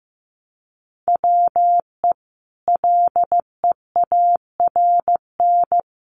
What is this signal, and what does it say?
Morse code, heard as a keyed tone, about 15 wpm, 700 Hz. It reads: WE LEARN
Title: Sentences from Top 900 Words 15wpm